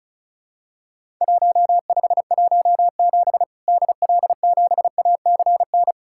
Morse code, heard as a keyed tone, about 36 wpm, 700 Hz. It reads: 1517 DL7ACN
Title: Worked All Europe DX Contest - QTC Exchange - 35wpm